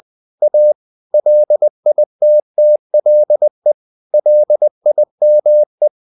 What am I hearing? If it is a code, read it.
Morse code, heard as a keyed tone, about 20 wpm, 600 Hz. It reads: A LITTLE LIME
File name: the old man and the sea / the old man and the sea part 1 20wpm